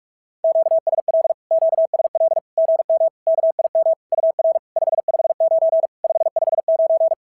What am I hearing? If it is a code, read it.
Morse code, heard as a keyed tone, about 46 wpm, 690 Hz. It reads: QSL QSL GM KIM UR 559 559
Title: SOTA QSO Practice 45wpm